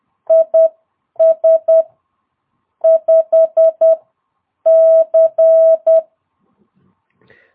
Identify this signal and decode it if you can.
Morse code, heard as a keyed tone, about 10 wpm, spaced as for 8 wpm, 650 Hz. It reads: IS 5C
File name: jh8cbh / 2022cw